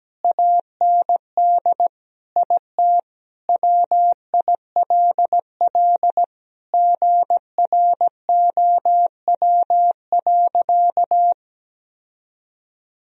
Morse code, heard as a keyed tone, about 17 wpm, 700 Hz. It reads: AND IT WILL GROW.
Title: Sentences from Top 1000 Words - Review Part 2 - 17wpm